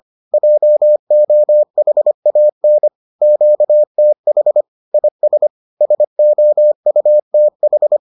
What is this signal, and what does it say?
Morse code, heard as a keyed tone, about 25 wpm, 600 Hz. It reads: JOHAN QTH IS SOUTH
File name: 25wpm-6